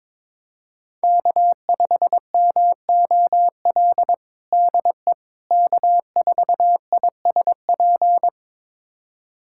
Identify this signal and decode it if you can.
Morse code, heard as a keyed tone, about 22 wpm, 700 Hz. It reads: K5MOL DE K4ISP
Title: USA Callsign DE Callsign - 5 Characters - 22wpm